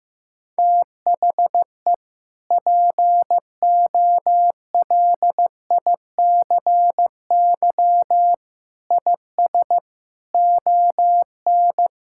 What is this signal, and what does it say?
Morse code, heard as a keyed tone, about 15 wpm, 700 Hz. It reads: THE POLICY IS ON
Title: Sentences from Top 1000 Words - Review Part 5 - 15wpm